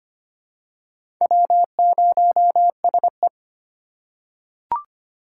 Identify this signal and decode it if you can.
Morse code, heard as a keyed tone, about 25 wpm, 700 Hz. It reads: W0SE
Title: USA Call Signs - 3x Character Spacing - 25wpm